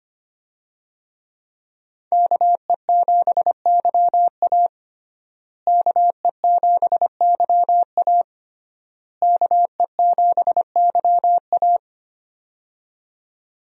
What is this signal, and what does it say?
Morse code, heard as a keyed tone, about 25 wpm, 700 Hz. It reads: KE7YA KE7YA KE7YA
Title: USA Call Signs - 5 Characters - Repeated 3x - 25wpm